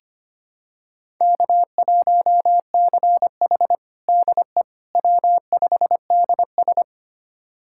Morse code, heard as a keyed tone, about 25 wpm, 700 Hz. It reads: K1CH DE W5DS
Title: USA Callsign DE Callsign - 4 Characters - 25wpm